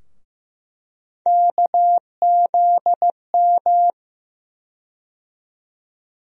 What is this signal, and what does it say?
Morse code, heard as a keyed tone, about 15 wpm, 700 Hz. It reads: KZM
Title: USA Call Sign Postfixes - 3 Letters 15wpm